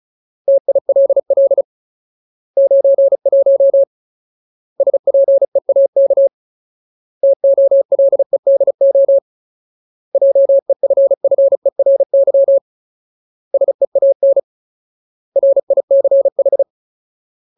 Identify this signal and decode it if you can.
Morse code, heard as a keyed tone, about 34 wpm, 550 Hz. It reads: TILL 91 SPEAK TOLEDO JEFFERY SEAN RICH
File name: Ultimate Word List / 35WPM Ultimate